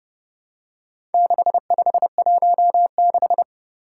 Morse code, heard as a keyed tone, about 30 wpm, 700 Hz. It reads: 6516
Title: Four Challenging Letter-Number Combinations 30wpm